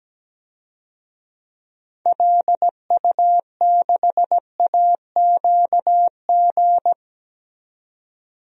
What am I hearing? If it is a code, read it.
Morse code, heard as a keyed tone, about 17 wpm, 700 Hz. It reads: LU6AQG
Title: DXCC-100 Call Signs - 6 Characters - 17wpm